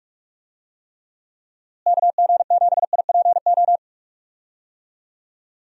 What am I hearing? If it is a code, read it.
Morse code, heard as a keyed tone, about 46 wpm, 700 Hz. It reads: KG7IPO
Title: USA Call Signs - 6 Characters - Repeated 2x - 45wpm